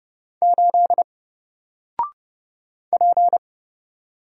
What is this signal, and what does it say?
Morse code, heard as a keyed tone, about 30 wpm, 700 Hz. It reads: 8 P
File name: Single-Letter-Number-30wpm